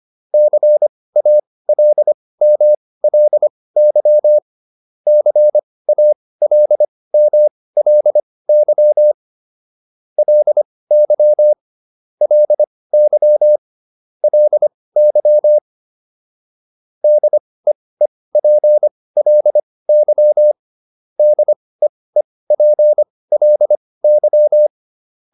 Morse code, heard as a keyed tone, about 25 wpm, spaced as for 18 wpm, 600 Hz. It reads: CALMLY CALMLY LY LY LY DEEPLY DEEPLY